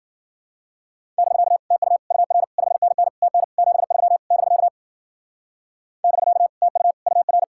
Morse code, heard as a keyed tone, about 50 wpm, 700 Hz. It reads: <BK> TU UR 5NN TN 73 <BK> <BK> TU UR
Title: Speed-Racing Common QSO Phrases - 40wpm